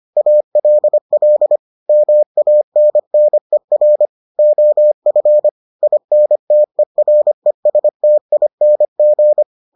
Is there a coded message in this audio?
Yes: Morse code from 0.2 to 9.4 s, heard as a keyed tone, about 25 wpm, 600 Hz. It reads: ALL MANNER OF INTERESTING